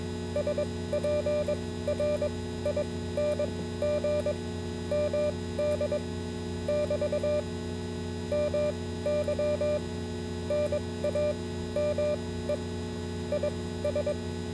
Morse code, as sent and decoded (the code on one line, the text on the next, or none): ... .--. .-. .. -. --. / -- -.. / -...- / -- -.-- / -. .- -- . / .. ...
SPRING MD <BT> MY NAME IS